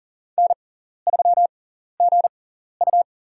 Morse code, heard as a keyed tone, about 40 wpm, spaced as for 14 wpm, 700 Hz. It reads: N3GU
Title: USA Call Signs - 4 Characters - 6x Character Spacing - 40wpm